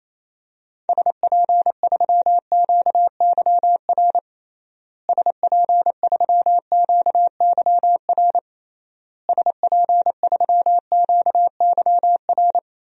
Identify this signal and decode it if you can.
Morse code, heard as a keyed tone, about 28 wpm, 700 Hz. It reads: SP3QYR SP3QYR SP3QYR
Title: DXCC-100 Call Signs - 6 Characters - Repeated 3x - 28wpm